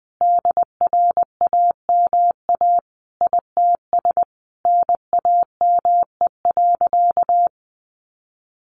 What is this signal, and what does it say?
Morse code, heard as a keyed tone, about 20 wpm, 700 Hz. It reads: DRAMA ITS NAME.
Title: Sentences from Top 1500 Words 20wpm